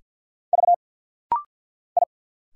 Morse code, heard as a keyed tone, about 46 wpm, 700 Hz. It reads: V I
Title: Single Letters - Rapid-Fire 50wpm